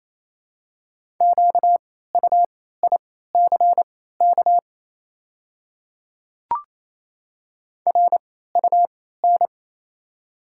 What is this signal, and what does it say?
Morse code, heard as a keyed tone, about 28 wpm, spaced as for 16 wpm, 700 Hz. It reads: QUICK RUN
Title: 3x Character Spacing - Top 500 Words - 28wpm